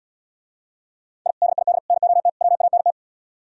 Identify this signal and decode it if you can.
Morse code, heard as a keyed tone, about 77 wpm, 700 Hz. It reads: I DID NOT WANT